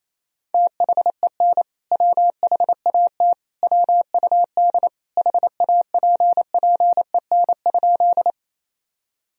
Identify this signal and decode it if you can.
Morse code, heard as a keyed tone, about 28 wpm, 700 Hz. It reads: THEN WHAT WUD HAPPEN?
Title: Sentences from Top 400 Words and Abbreviations 28wpm